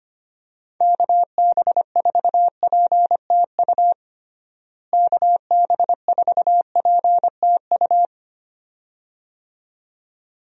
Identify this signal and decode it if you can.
Morse code, heard as a keyed tone, about 25 wpm, 700 Hz. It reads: KB4PTU KB4PTU
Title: USA Call Signs - 6 Characters - Repeated 2x - 25wpm